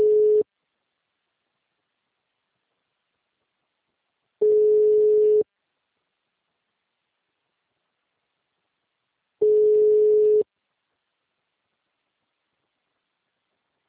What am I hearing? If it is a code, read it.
Ringback tone (Europe, 425 Hz).